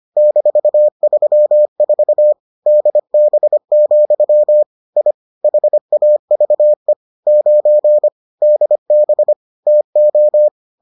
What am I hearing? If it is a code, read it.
Morse code, heard as a keyed tone, about 25 wpm, 600 Hz. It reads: -34 DB, I HAVE 9 DB TO